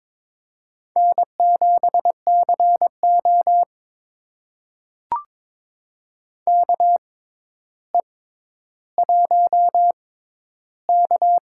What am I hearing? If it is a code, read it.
Morse code, heard as a keyed tone, about 22 wpm, spaced as for 11 wpm, 700 Hz. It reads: N7CO KE1K
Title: USA Call Signs - 4 Characters - 6x Character Spacing - 22wpm